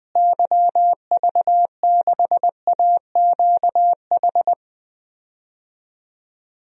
Morse code, heard as a keyed tone, about 20 wpm, 700 Hz. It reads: YV6AQH
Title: DXCC-100 Call Signs - 6 Characters - Repeated 2x - 20wpm